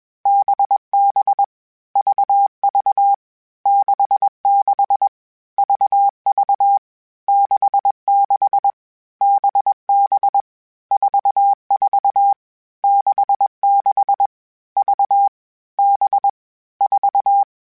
Morse code, heard as a keyed tone, about 21 wpm, 800 Hz. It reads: BB VV 66 VV 66 BB 44 66 V B 4